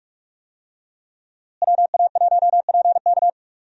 Morse code, heard as a keyed tone, about 46 wpm, 700 Hz. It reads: WA1PK
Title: USA Call Signs - 5 Characters - Repeated 2x - 45wpm